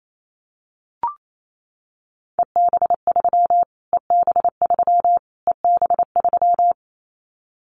Morse code, heard as a keyed tone, about 28 wpm, 700 Hz. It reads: EB3 EB3 EB3
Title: Callsign Prefixes for Top 25 DXCC - Repeated 3x - 28wpm